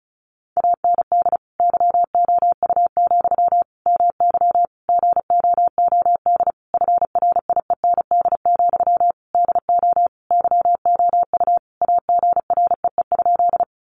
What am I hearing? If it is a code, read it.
Morse code, heard as a keyed tone, about 34 wpm, 700 Hz. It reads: AND YOU, MY GOOD FRIEND, DO YOU AGREE?
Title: Sentences from Top 1100 Words 35wpm